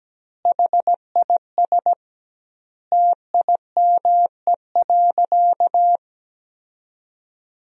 Morse code, heard as a keyed tone, about 17 wpm, 700 Hz. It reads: HIS TIME.
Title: 2x Word-Spacing - Sentences from Top 400 Words - 17wpm